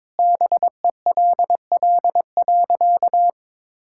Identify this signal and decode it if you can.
Morse code, heard as a keyed tone, about 22 wpm, 700 Hz. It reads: BELL.
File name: practice / 22wpm / Sentences with 2-Word Phrases - 22wpm